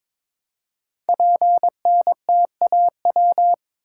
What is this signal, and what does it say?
Morse code, heard as a keyed tone, about 22 wpm, 700 Hz. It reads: PNTAW
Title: Code Groups - 5 Letters - 22wpm